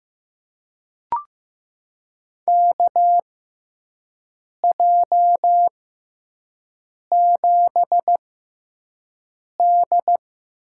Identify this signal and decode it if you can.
Morse code, heard as a keyed tone, about 15 wpm, spaced as for 5 wpm, 700 Hz. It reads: KJ7D
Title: USA Call Signs - 6x Character Spacing - 15wpm